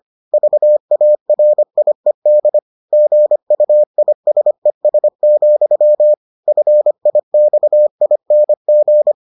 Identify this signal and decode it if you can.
Morse code, heard as a keyed tone, about 25 wpm, 600 Hz. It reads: VARIED GUISES, FIXING